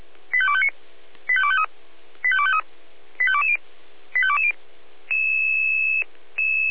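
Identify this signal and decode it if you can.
Five-tone selective call (ZVEI), heard as quick runs of five tones, tones of about 70 ms. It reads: 85259 85253 85253 85220 85220
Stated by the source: ZVEI five-tone selective call received on an SDR.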